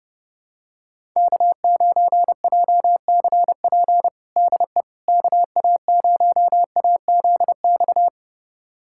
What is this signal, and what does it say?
Morse code, heard as a keyed tone, about 31 wpm, 700 Hz. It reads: K9JCP DE KA0AZX
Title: USA Callsign DE Callsign 30wpm